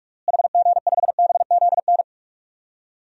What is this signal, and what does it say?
Morse code, heard as a keyed tone, about 46 wpm, 700 Hz. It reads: HG5BZN
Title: DXCC-100 Call Signs - 6 Characters - Repeated 3x - 45wpm